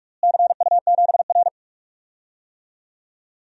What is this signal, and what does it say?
Morse code, heard as a keyed tone, about 45 wpm, 700 Hz. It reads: CU7R
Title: DXCC-100 Call Signs - 4 Characters - Repeated 2x - 45wpm